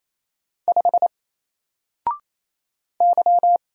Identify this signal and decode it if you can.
Morse code, heard as a keyed tone, about 27 wpm, 700 Hz. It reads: 5 Y
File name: Single-Letter-Number-28wpm